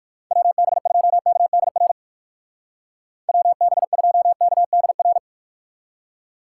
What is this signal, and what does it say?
Morse code, heard as a keyed tone, about 56 wpm, 700 Hz. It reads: WB2KDR WB2KDR
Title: Speed Racing - USA 2x3 Call Signs 45wpm